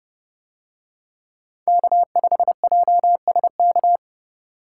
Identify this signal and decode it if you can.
Morse code, heard as a keyed tone, about 28 wpm, 700 Hz. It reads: K5JSK